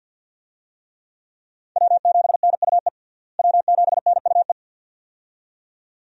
Morse code, heard as a keyed tone, about 53 wpm, 700 Hz. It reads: W7NUE W7NUE
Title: USA Call Signs - 5 Characters - Repeated 2x - 50wpm